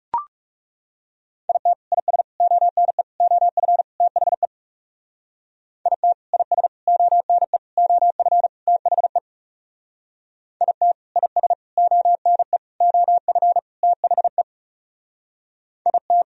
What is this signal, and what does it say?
Morse code, heard as a keyed tone, about 40 wpm, 700 Hz. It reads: IT IS ONE OF THE IT IS ONE OF THE IT IS ONE OF THE IT